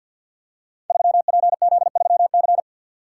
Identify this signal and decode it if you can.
Morse code, heard as a keyed tone, about 47 wpm, 700 Hz. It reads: 3PZ3C
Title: Code Groups - 5 Letter:Numbers - 50wpm